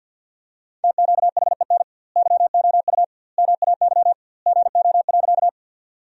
Morse code, heard as a keyed tone, about 50 wpm, 700 Hz. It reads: TQHEN YOU MAY GO.
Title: 2x Word-Spacing - Sentences from Top 100 Words - 50wpm